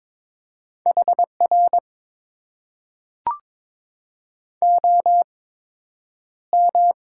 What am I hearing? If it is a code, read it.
Morse code, heard as a keyed tone, about 22 wpm, spaced as for 9 wpm, 700 Hz. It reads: HR OM
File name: Top 90 QSO Elements - 8x Character Spacing - 22wpm